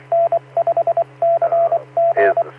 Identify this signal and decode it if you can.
Morse code, heard as a keyed tone, about 24 wpm, 680 Hz, with radio noise. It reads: N5CG
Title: Saline, AK SKYWARN Repeater System - W5RHS